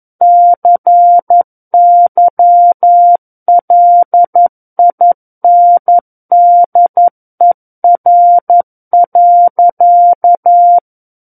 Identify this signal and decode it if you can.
Morse code, heard as a keyed tone, about 11 wpm, 700 Hz. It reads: CYLINDER.
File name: Book One / 02-Chapter 2 - 11 WPM